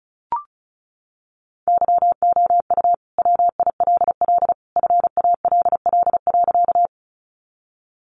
Morse code, heard as a keyed tone, about 34 wpm, 700 Hz. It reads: YOU WILL FALL.
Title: Sentences from Top 500 Words 35wpm